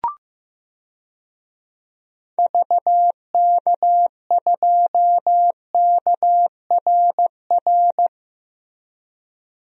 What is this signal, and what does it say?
Morse code, heard as a keyed tone, about 15 wpm, 700 Hz. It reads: VK2KRR
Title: DXCC-100 Call Signs - 6 Characters - Repeated 2x - 15wpm